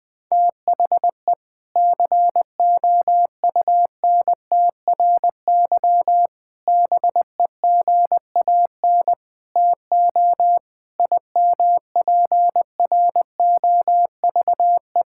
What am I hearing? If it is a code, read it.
Morse code, heard as a keyed tone, about 20 wpm, 700 Hz. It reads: THE COUNTRY BEGAN TO IMPROVE
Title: Sentences from Top 1000 Words - Review Part 5 - 20wpm